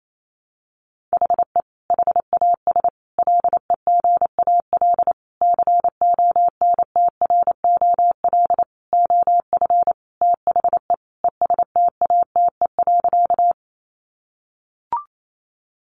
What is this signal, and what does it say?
Morse code, heard as a keyed tone, about 28 wpm, 700 Hz. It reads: HE HAS LEGAL CONTROL OF THE ESTATE.
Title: Sentences from Top 1200 Words 28wpm